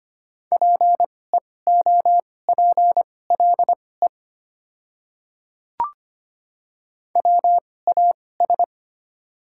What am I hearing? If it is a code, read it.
Morse code, heard as a keyed tone, about 25 wpm, spaced as for 18 wpm, 690 Hz. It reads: PEOPLE WAS
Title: 2x Character Spacing - Top 100 Words - 25wpm